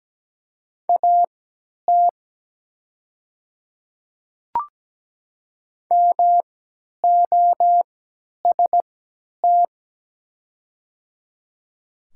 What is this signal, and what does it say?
Morse code, heard as a keyed tone, about 17 wpm, spaced as for 10 wpm, 700 Hz. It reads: AT MOST
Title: 3x Character Spacing - Top 200 Words - 17wpm